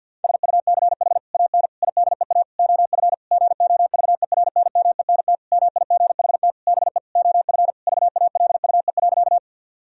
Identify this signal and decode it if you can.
Morse code, heard as a keyed tone, about 50 wpm, 700 Hz. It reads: SUCH AN IDEA OF GOVERNMENT MIGHT BE OF VALUE.
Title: Sentences from Top 1000 Words - Review Part 3 - 50wpm